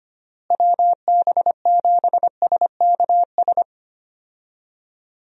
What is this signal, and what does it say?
Morse code, heard as a keyed tone, about 25 wpm, 700 Hz. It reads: WB7SKS
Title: USA Call Signs - 6 Characters - Repeated 3x - 25wpm